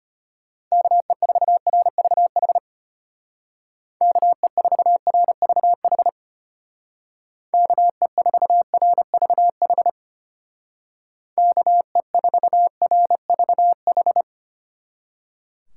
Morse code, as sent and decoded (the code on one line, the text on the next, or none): -.- . ....- .-. ...- .... / -.- . ....- .-. ...- .... / -.- . ....- .-. ...- .... / -.- . ....- .-. ...- ....
KE4RVH KE4RVH KE4RVH KE4RVH